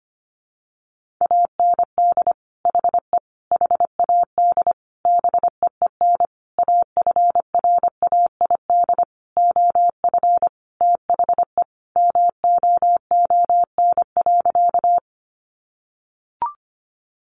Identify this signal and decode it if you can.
Morse code, heard as a keyed tone, about 25 wpm, 700 Hz. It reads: AND HE HAD BEEN AFRAID OF THE MOON.
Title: Sentences from Top 1500 Words 25wpm